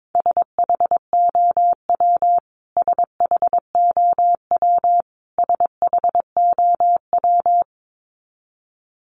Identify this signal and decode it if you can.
Morse code, heard as a keyed tone, about 22 wpm, 700 Hz. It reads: SHOW SHOW SHOW
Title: Top 200 Words Repeated 22wpm